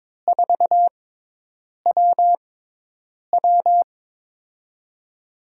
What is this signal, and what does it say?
Morse code, heard as a keyed tone, about 22 wpm, spaced as for 8 wpm, 700 Hz. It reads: 4WW
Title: USA Call Signs - 6x Character Spacing - 22wpm